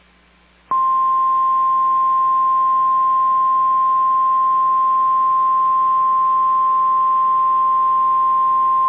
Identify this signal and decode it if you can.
Emergency Alert System (SAME) tones (the 1050 Hz attention tone). It is heard from 1.1 to 7.8 s.
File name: EAS Logs / KMNR